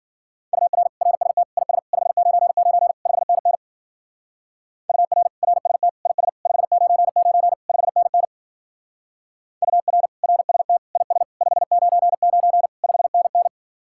Morse code, heard as a keyed tone, about 47 wpm, 700 Hz. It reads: UR RST IS 599 5NN UR RST IS 599 5NN UR RST IS 599 5NN
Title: Speed-Racing Common QSO Phrases - 40wpm